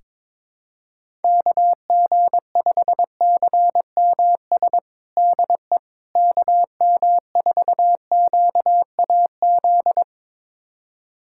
Morse code, heard as a keyed tone, about 22 wpm, 700 Hz. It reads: KG5CMS DE KM4QAZ
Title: USA Callsign DE Callsign - 6 Characters - 22wpm